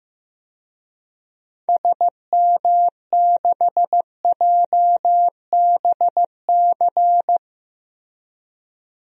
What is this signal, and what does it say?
Morse code, heard as a keyed tone, about 15 wpm, 700 Hz. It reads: SM6JBC
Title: DXCC-100 Call Signs - 6 Characters - Repeated 2x - 15wpm